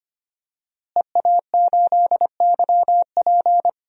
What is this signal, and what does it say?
Morse code, heard as a keyed tone, about 25 wpm, 700 Hz. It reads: EA8YP
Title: DXCC-100 Call Signs - 5 Characters - Repeated 3x - 25wpm